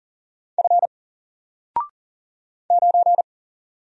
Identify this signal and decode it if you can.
Morse code, heard as a keyed tone, about 40 wpm, 700 Hz. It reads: F 9
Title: Single Letter-Number Alt 40wpm